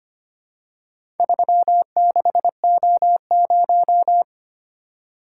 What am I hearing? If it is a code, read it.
Morse code, heard as a keyed tone, about 25 wpm, 700 Hz. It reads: 36O0
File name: practice / 25wpm / Four Letter-Number Combinations - 25wpm